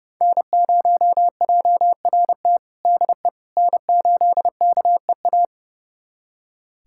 Morse code, heard as a keyed tone, about 30 wpm, 700 Hz. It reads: N0JRT DE N8KEA
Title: USA Callsign DE Callsign - 5 Characters - 30wpm